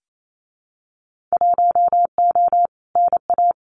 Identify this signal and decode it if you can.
Morse code, heard as a keyed tone, about 28 wpm, 700 Hz. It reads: 1O NA
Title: Winter Field Day Exchanges 28wpm